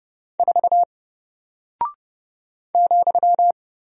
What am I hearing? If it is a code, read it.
Morse code, heard as a keyed tone, about 30 wpm, 700 Hz. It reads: 4 ,
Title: Single Character (Letter:Number:Punctuation:Prosign) Alt 30wpm